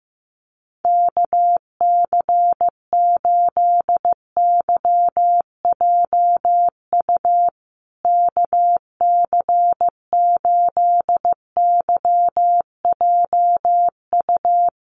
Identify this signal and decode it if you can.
Morse code, heard as a keyed tone, about 14 wpm, 700 Hz. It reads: KC8YJU KC8YJU
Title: USA Call Signs Repeated 3x 15wpm